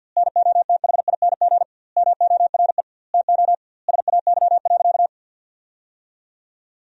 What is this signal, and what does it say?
Morse code, heard as a keyed tone, about 49 wpm, 700 Hz. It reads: NOTHING MORE TO SAY.
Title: 2x Word-Spacing - Sentences from Top 500 Words - 50wpm